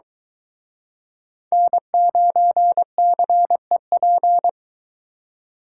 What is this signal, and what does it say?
Morse code, heard as a keyed tone, about 23 wpm, 700 Hz. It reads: N9CEP